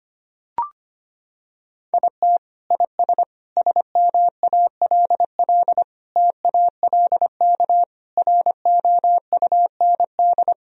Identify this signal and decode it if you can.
Morse code, heard as a keyed tone, about 25 wpm, 700 Hz. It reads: IT IS SMALL TALK ROUND
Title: Sentences from Top 1000 Words - Review Part 1 - 25wpm